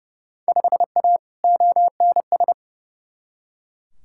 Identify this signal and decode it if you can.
Morse code, heard as a keyed tone, about 31 wpm, 700 Hz. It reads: 5A ONS
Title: Field Day Exchanges Encore - 30wpm